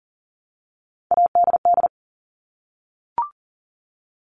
Morse code, heard as a keyed tone, about 38 wpm, 700 Hz. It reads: ADD
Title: Multi-letter Phonemes 40wpm